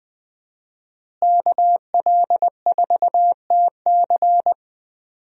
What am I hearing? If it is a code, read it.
Morse code, heard as a keyed tone, about 20 wpm, 700 Hz. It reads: KL4TC
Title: USA Call Signs - 5 Characters - Repeated 3x - 20wpm